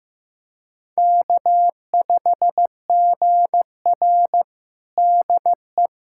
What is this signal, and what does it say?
Morse code, heard as a keyed tone, about 15 wpm, 700 Hz. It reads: K5GR DE
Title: USA Callsign DE Callsign 15wpm